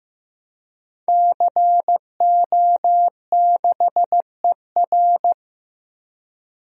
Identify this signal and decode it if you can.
Morse code, heard as a keyed tone, about 15 wpm, 700 Hz. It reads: CO6ER
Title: DXCC-100 Call Signs - 5 Characters - Repeated 3x - 15wpm